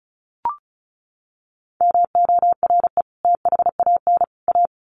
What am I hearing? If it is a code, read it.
Morse code, heard as a keyed tone, about 34 wpm, 700 Hz. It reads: MORE THAN A
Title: Sets of 3 Words 35wpm